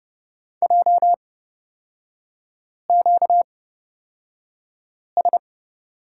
Morse code, heard as a keyed tone, about 31 wpm, spaced as for 5 wpm, 700 Hz. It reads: JQS
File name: Lesson 38e - Letter q - Three Characters 30wpm